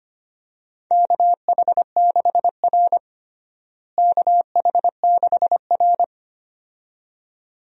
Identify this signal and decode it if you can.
Morse code, heard as a keyed tone, about 25 wpm, 700 Hz. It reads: KH6R KH6R
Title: DXCC-100 Call Signs - 4 Characters - Repeated 2x - 25wpm